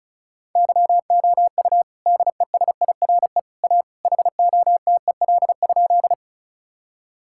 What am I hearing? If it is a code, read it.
Morse code, heard as a keyed tone, about 34 wpm, 700 Hz. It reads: YOU DESIRE A HOTEL?
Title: Sentences from Top 1600 Words 35wpm